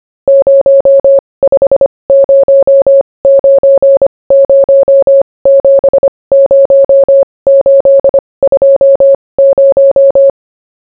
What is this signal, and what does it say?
Morse code, heard as a keyed tone, about 25 wpm, spaced as for 20 wpm, 550 Hz. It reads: 0509070820